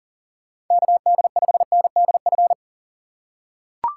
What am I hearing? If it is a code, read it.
Morse code, heard as a keyed tone, about 37 wpm, 700 Hz. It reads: KD5NDF
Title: USA Call Signs - 6 Characters - 2x Character Spacing - 40wpm